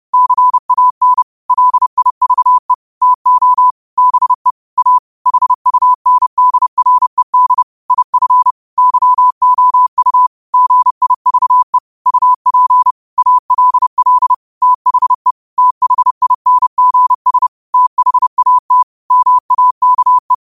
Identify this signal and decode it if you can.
Morse code, heard as a keyed tone, about 30 wpm, 1000 Hz. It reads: CAN LIVE TO BE A HUNDRED IF YOU GIVE UP ALL THE THINGS THAT MAKE
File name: qotdfiles / 30wpm